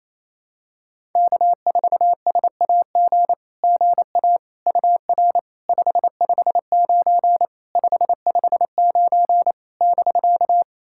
Morse code, heard as a keyed tone, about 27 wpm, 690 Hz. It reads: K4SAG GA UR 559 559 <BK>
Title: SOTA QSO Practice 28wpm